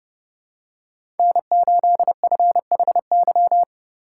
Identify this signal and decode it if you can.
Morse code, heard as a keyed tone, about 32 wpm, 700 Hz. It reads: N8FHY
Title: USA Call Signs - 5 Characters - 3x Character Spacing - 30wpm